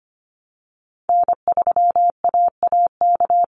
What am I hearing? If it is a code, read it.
Morse code, heard as a keyed tone, about 25 wpm, 700 Hz. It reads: N3AAK
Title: Call Signs Repeated 3x - 25wpm